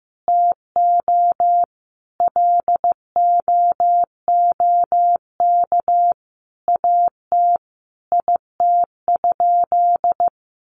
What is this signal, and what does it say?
Morse code, heard as a keyed tone, about 15 wpm, 700 Hz. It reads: TO LOOK AT IT?
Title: Sentences from Top 100 Words 15wpm